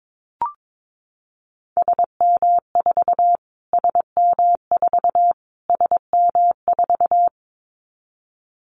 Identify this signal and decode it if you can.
Morse code, heard as a keyed tone, about 22 wpm, 700 Hz. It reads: SM4 SM4 SM4
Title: Callsign Prefixes for Top 25 DXCC - Repeated 3x - 22wpm